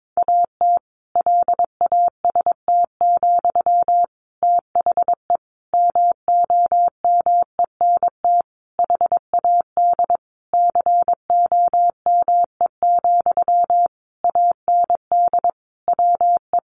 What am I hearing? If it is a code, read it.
Morse code, heard as a keyed tone, about 22 wpm, 700 Hz. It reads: AT LAST, THE MOMENT HAD COME, AND WE
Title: Sentences from Top 1500 Words 22wpm